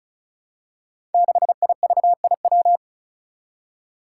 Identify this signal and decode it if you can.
Morse code, heard as a keyed tone, about 36 wpm, 700 Hz. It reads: 6IVIW
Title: Five Challenging Letter-Number Combinations 35wpm